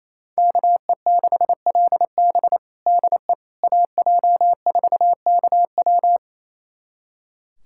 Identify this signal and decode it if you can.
Morse code, heard as a keyed tone, about 28 wpm, 700 Hz. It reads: KE6LB DE AJ4KW
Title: USA Callsign DE Callsign - 5 Characters - 28wpm